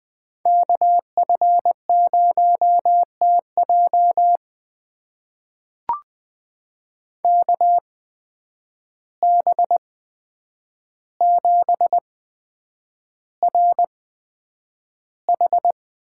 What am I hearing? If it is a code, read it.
Morse code, heard as a keyed tone, about 21 wpm, spaced as for 9 wpm, 700 Hz. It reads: KF0TJ KB7RH
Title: USA Call Signs - 5 Characters - 8x Character Spacing - 20wpm